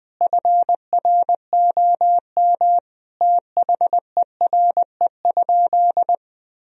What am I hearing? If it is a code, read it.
Morse code, heard as a keyed tone, about 20 wpm, 700 Hz. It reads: FROM THERE?
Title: Sentences from Top 1000 Words - Review Part 1 - 20wpm